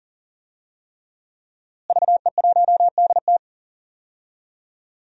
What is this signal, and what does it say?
Morse code, heard as a keyed tone, about 38 wpm, 700 Hz. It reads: VE1DT
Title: DXCC-100 Call Signs - 5 Characters - Repeated 3x - 40wpm